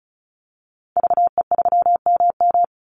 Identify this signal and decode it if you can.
Morse code, heard as a keyed tone, about 34 wpm, 700 Hz. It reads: VE3MM
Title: Call Signs - 35wpm